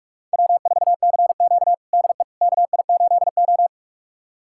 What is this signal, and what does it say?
Morse code, heard as a keyed tone, about 42 wpm, 700 Hz. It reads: W4CQ DE KI8O
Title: USA Callsign DE Callsign - 4 Characters - 45wpm